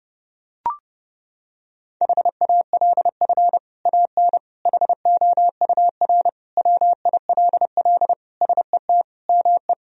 Morse code, heard as a keyed tone, about 32 wpm, 700 Hz. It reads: HALF AN HOUR WILL SET ME
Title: Sentences from Top 1000 Words - Review Part 1 - 30wpm